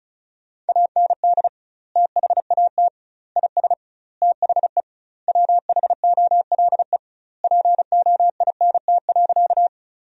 Morse code, heard as a keyed tone, about 34 wpm, 700 Hz. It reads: AND THAT IS THE WHOLE POINT.
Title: 2x Word-Spacing - Sentences from Top 400 Words - 35wpm